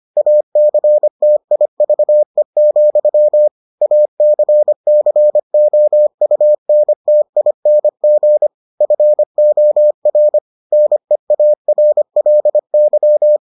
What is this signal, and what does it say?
Morse code, heard as a keyed tone, about 25 wpm, 600 Hz. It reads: ACTIVE, ACCOUNTING FOR NEARLY